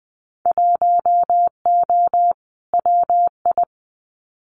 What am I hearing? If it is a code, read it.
Morse code, heard as a keyed tone, about 20 wpm, 700 Hz. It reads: 1O WI
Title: Winter Field Day Exchanges 20wpm